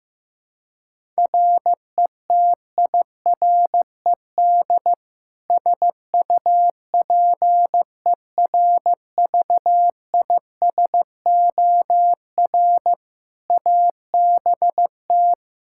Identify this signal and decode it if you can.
Morse code, heard as a keyed tone, about 16 wpm, 700 Hz. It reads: RETIRED SUPERVISOR ABT
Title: Practice QSO - Exchange 3 - More Info - 15wpm